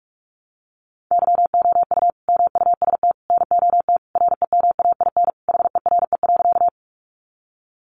Morse code, heard as a keyed tone, about 44 wpm, 700 Hz. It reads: YOU MUST NOT REMAIN HERE.
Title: Sentences from Top 1200 Words 45wpm